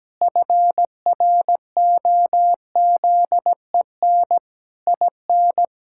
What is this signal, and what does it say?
Morse code, heard as a keyed tone, about 17 wpm, 700 Hz. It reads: FROZEN IN
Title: Pangram Challenge v2 - 17wpm